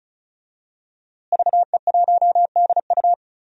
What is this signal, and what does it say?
Morse code, heard as a keyed tone, about 36 wpm, 700 Hz. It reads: VE1DU
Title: DXCC-100 Call Signs - 5 Characters - Repeated 2x - 35wpm